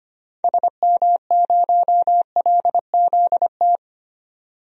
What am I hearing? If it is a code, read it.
Morse code, heard as a keyed tone, about 25 wpm, 700 Hz. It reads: SM0LZT
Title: DXCC-100 Call Signs - 6 Characters - 25wpm